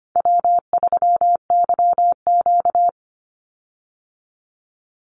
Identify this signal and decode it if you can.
Morse code, heard as a keyed tone, about 25 wpm, 700 Hz. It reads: W3YQ
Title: Call Signs - 25wpm